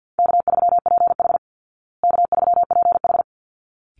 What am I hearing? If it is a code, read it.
Morse code, heard as a keyed tone, about 50 wpm, 700 Hz. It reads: K3PH K3PH
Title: Call Signs Repeated 3x - 50wpm